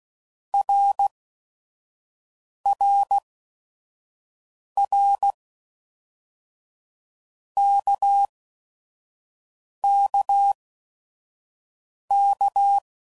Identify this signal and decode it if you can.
Morse code, heard as a keyed tone, about 16 wpm, spaced as for 5 wpm, 800 Hz. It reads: RRR KKK
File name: Basic Letter Groups By7Groups / RKPX / RKPX@5 f15